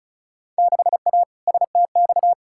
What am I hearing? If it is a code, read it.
Morse code, heard as a keyed tone, about 36 wpm, 700 Hz. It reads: 6A STX
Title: Field Day Exchanges - 35wpm